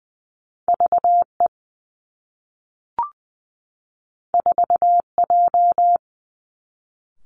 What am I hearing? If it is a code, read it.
Morse code, heard as a keyed tone, about 20 wpm, 700 Hz. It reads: V E 4 J
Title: Two Challenging Letter-Number Combinations 20wpm